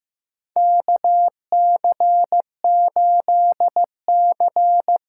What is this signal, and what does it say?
Morse code, heard as a keyed tone, about 15 wpm, 700 Hz. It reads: KC8C
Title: USA Call Signs - 6 Characters - 6x Character Spacing - 15wpm